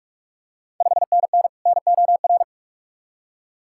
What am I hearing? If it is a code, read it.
Morse code, heard as a keyed tone, about 44 wpm, 690 Hz. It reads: 5NN NOR
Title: Ontario QSO Party Exchange 45wpm